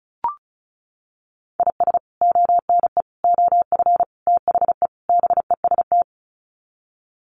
Morse code, heard as a keyed tone, about 34 wpm, 700 Hz. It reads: IS ONE OF THE BEST
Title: Sets of 5 Words 35wpm